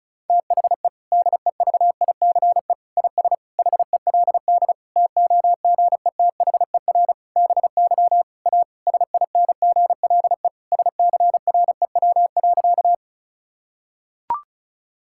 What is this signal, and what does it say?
Morse code, heard as a keyed tone, about 34 wpm, 700 Hz. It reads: THE DEVICE IS HELD TOGETHER BY A SINGLE SCREW.
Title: Sentences from Top 1600 Words 35wpm